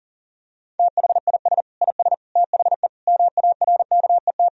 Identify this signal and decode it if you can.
Morse code, heard as a keyed tone, about 37 wpm, 700 Hz. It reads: THIS IS THE MARKET
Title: Sentences from Top 1000 Words - Review Part 3 - 40wpm